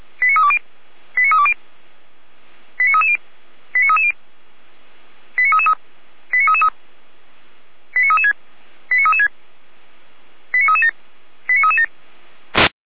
Five-tone selective call (ZVEI), heard as quick runs of five tones, tones of about 70 ms. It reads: 89320 89320 89330 89330 89333 89336 89336 89337 89337
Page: audio.f85360.feuerwehr-orlamuende.org